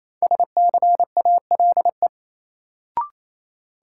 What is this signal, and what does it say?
Morse code, heard as a keyed tone, about 28 wpm, 700 Hz. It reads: SCALE
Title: Sentences from Top 1000 Words - Review Part 4 - New Word Drill - 28wpm